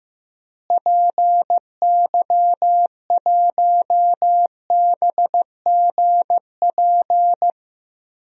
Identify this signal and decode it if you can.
Morse code, heard as a keyed tone, about 15 wpm, 700 Hz. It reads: PY1BGP
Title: DXCC-100 Call Signs - 6 Characters - Repeated 3x - 15wpm